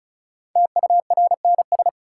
Morse code, heard as a keyed tone, about 36 wpm, 700 Hz. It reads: TURNS
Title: Sets of 5 Words Encore - New Word Drill - 35wpm